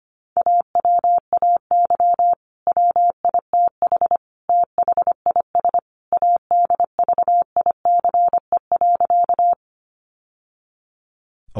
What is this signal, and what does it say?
Morse code, heard as a keyed tone, about 25 wpm, 700 Hz. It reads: AWAY WITH THIS ADVICE.
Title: Sentences from Top 1100 Words 25wpm